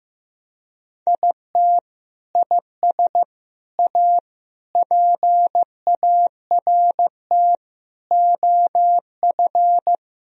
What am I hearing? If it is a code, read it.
Morse code, heard as a keyed tone, about 15 wpm, 700 Hz. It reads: IT IS A PART OF
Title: Sentences from Top 1000 Words - Review Part 4 - 15wpm